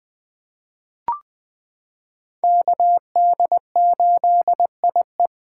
Morse code, heard as a keyed tone, about 21 wpm, 700 Hz. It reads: KD8IE
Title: 2023 POTA Activator Call Signs - 20wpm